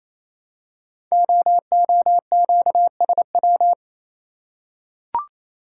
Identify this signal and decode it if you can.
Morse code, heard as a keyed tone, about 28 wpm, 700 Hz. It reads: OOQSW
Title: Code Groups - 5 Letters - 28wpm